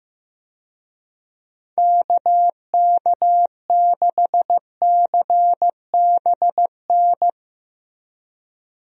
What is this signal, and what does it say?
Morse code, heard as a keyed tone, about 15 wpm, 700 Hz. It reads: KK6CBN
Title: USA Call Signs - 6 Characters - Repeated 2x - 15wpm